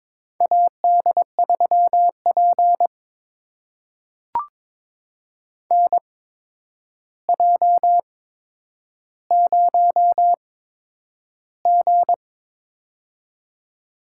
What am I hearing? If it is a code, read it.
Morse code, heard as a keyed tone, about 22 wpm, spaced as for 9 wpm, 700 Hz. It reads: AD3P NJ0G
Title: USA Call Signs - 4 Characters - 8x Character Spacing - 22wpm